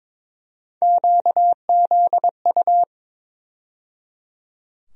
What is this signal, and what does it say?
Morse code, heard as a keyed tone, about 22 wpm, 700 Hz. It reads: QZU
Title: USA Call Sign Postfixes - 3 Letters 22wpm